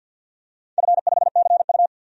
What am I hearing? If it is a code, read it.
Morse code, heard as a keyed tone, about 50 wpm, 700 Hz. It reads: F5CU
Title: DXCC-100 Call Signs - 4 Characters - 50wpm